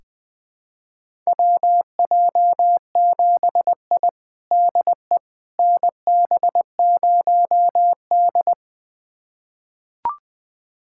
Morse code, heard as a keyed tone, about 20 wpm, 700 Hz. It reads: WJ7I DE NB0D
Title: USA Callsign DE Callsign - 4 Characters - 20wpm